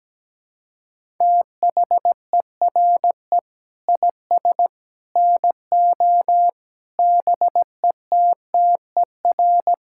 Morse code, heard as a keyed tone, about 17 wpm, 700 Hz. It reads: THERE IS NO BETTER
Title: Sentences from Top 1000 Words - Review Part 5 - 17wpm